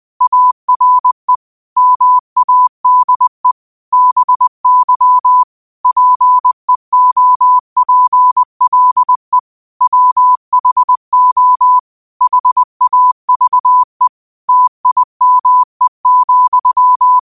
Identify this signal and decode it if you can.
Morse code, heard as a keyed tone, about 20 wpm, 1000 Hz. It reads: ARE MADE BY PEOPLE WHO HAVE TIME,